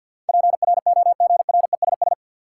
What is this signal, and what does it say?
Morse code, heard as a keyed tone, about 52 wpm, 700 Hz. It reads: PROGRESS
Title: Sentences from Top 1400 Words - New Word Drill - 50wpm